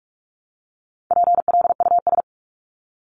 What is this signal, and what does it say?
Morse code, heard as a keyed tone, about 44 wpm, 700 Hz. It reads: PLUS
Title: Top 1001-1250 Words 45wpm